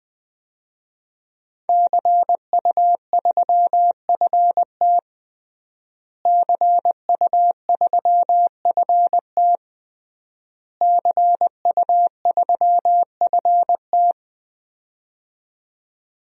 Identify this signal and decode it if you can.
Morse code, heard as a keyed tone, about 20 wpm, 700 Hz. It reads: CU3FT CU3FT CU3FT
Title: DXCC-100 Call Signs - 5 Characters - Repeated 3x - 20wpm